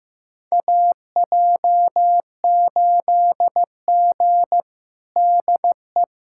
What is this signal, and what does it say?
Morse code, heard as a keyed tone, about 15 wpm, 700 Hz. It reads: AJ8G DE
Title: USA Callsign DE Callsign - 4 Characters - 15wpm